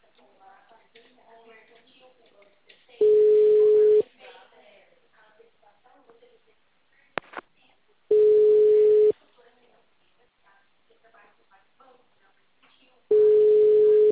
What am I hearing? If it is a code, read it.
Ringback tone (Europe, 425 Hz), with speech.